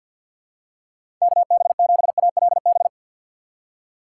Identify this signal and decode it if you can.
Morse code, heard as a keyed tone, about 46 wpm, 700 Hz. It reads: KB7ALB
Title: USA Call Signs - 6 Characters - 4x Character Spacing - 50wpm